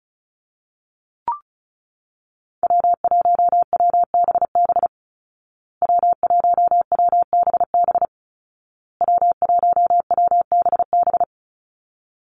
Morse code, heard as a keyed tone, about 34 wpm, 700 Hz. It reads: W1WBB W1WBB W1WBB
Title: Call Signs Repeated 2x - 35wpm